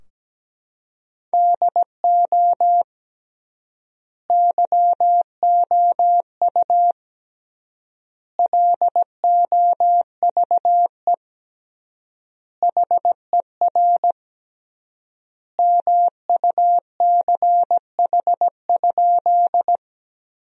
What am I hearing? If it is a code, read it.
Morse code, heard as a keyed tone, about 17 wpm, 700 Hz. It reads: DO YOU LOVE HER MUCH?
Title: 3x Word-Spacing - Sentences from Top 400 Words - 17wpm